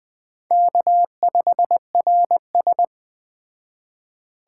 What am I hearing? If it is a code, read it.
Morse code, heard as a keyed tone, about 20 wpm, 700 Hz. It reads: K5RS
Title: Speed Racing - USA Call Signs 17wpm